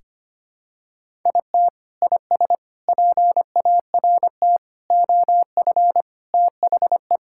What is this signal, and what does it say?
Morse code, heard as a keyed tone, about 25 wpm, 700 Hz. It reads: IT IS PART OF THE
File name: practice / 25wpm / Sentences with 4-Word Phrases - 25wpm